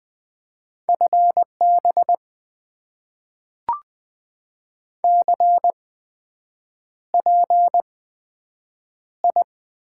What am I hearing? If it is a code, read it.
Morse code, heard as a keyed tone, about 20 wpm, spaced as for 7 wpm, 700 Hz. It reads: FB CPI
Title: Top 90 QSO Elements - 8x Character Spacing - 20wpm